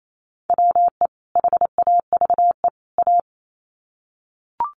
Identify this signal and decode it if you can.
Morse code, heard as a keyed tone, about 28 wpm, 700 Hz. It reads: WE HAVE A
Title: Sets of 3 Words 28wpm